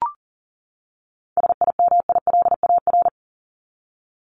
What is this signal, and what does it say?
Morse code, heard as a keyed tone, about 40 wpm, 700 Hz. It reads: SIMILAR
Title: LY Suffix - Top 100 Words with Base Words 40wpm